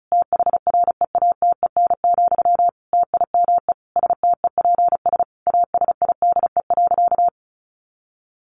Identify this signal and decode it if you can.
Morse code, heard as a keyed tone, about 34 wpm, 700 Hz. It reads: THREATEN, TIME STEPS ASIDE.
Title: A Sound Of Thunder 35wpm